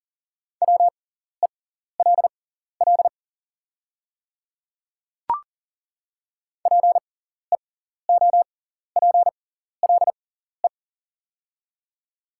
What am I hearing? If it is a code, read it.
Morse code, heard as a keyed tone, about 40 wpm, spaced as for 14 wpm, 700 Hz. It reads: WELL PEOPLE
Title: 6x Character Spacing - Top 300 Words - 40wpm